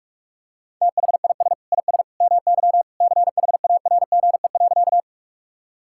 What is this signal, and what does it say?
Morse code, heard as a keyed tone, about 43 wpm, 690 Hz. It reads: THIS IS MY CHARGE.
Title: Sentences from Top 1000 Words - Review Part 6 - 45wpm